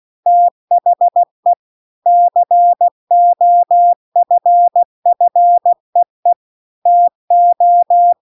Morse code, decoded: THE COFFEE TO

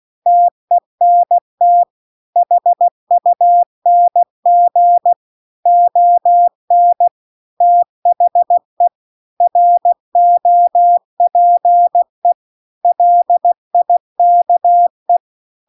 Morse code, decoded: TENT HUNG ON THE ROPE LIKE

